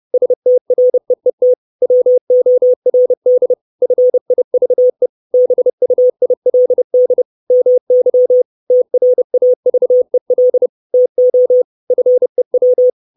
Morse code, heard as a keyed tone, about 30 wpm, 500 Hz. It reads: STREET WORD FIVE BUILD MY TRAVEL TO FEW
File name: 632 Words 30WPM